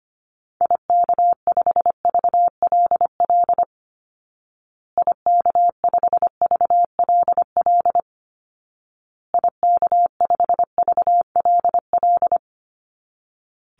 Morse code, heard as a keyed tone, about 25 wpm, 700 Hz. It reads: IK5VLL IK5VLL IK5VLL
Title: Call Signs Repeated 2x - 25wpm